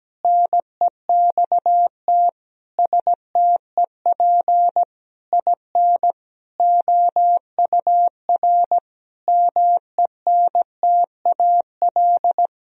Morse code, heard as a keyed tone, about 17 wpm, 700 Hz. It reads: NEXT STEP IN OUR MENTAL
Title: Sentences from Top 1000 Words - Review Part 4 - 17wpm